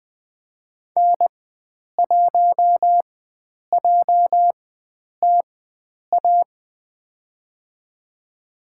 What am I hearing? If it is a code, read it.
Morse code, heard as a keyed tone, about 20 wpm, spaced as for 9 wpm, 700 Hz. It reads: N1JTA
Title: USA Call Signs - 5 Characters - 4x Character Spacing - 20wpm